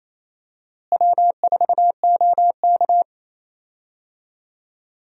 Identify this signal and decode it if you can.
Morse code, heard as a keyed tone, about 28 wpm, 700 Hz. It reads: W4OK
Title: USA Call Signs - 4 Characters - Repeated 3x - 28wpm